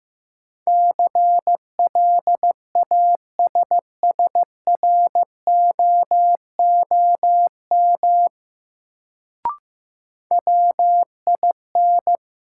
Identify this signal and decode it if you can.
Morse code, heard as a keyed tone, about 15 wpm, 700 Hz. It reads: CLASSROOM WIN
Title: Sentences from Top 1500 Words - New Word Drill - 15wpm